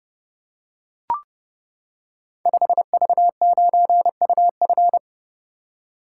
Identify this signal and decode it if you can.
Morse code, heard as a keyed tone, about 30 wpm, 700 Hz. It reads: 5V9UF
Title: Code Groups - 5 Letter:Numbers - 30wpm